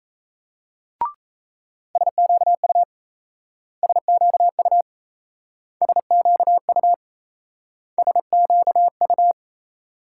Morse code, heard as a keyed tone, about 34 wpm, 700 Hz. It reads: SQU SQU SQU SQU
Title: Speed-Racing Multi-letter Phonemes - 28wpm